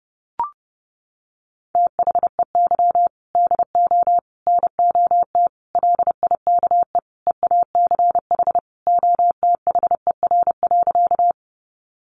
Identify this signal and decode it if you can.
Morse code, heard as a keyed tone, about 30 wpm, 700 Hz. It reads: THEY DO NOT LIKE EACH OTHER.
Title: Easy Phrases 30wpm